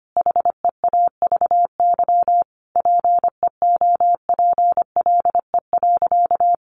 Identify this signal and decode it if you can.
Morse code, heard as a keyed tone, about 25 wpm, 700 Hz. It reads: HEAVY PEOPLE.
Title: Sentences from Top 1100 Words 25wpm